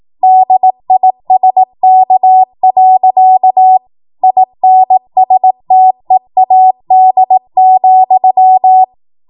Morse code, heard as a keyed tone, about 18 wpm, 750 Hz. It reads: DISK. INSTEAD,
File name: Morse / 18WPM